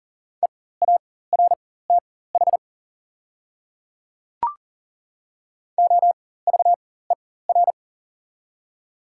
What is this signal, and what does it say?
Morse code, heard as a keyed tone, about 40 wpm, spaced as for 19 wpm, 700 Hz. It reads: EARTH OVER
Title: 4x Character Spacing - Top 500 Words - 40wpm